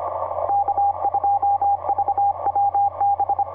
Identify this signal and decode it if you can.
Morse code, heard as a keyed tone, about 25 wpm, 800 Hz, with radio noise. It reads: K2VWB
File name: xw3dt dl6cw etc 30m cw 13jan2010 1553